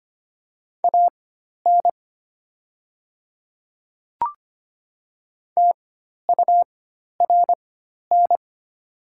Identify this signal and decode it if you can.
Morse code, heard as a keyed tone, about 25 wpm, spaced as for 12 wpm, 700 Hz. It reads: AN TURN
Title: 4x Character Spacing - Top 200 Words - 25wpm